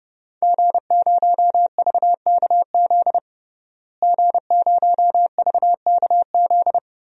Morse code, heard as a keyed tone, about 32 wpm, 700 Hz. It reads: G0VKZ G0VKZ
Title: DXCC-100 Call Signs - 5 Characters - Repeated 2x - 30wpm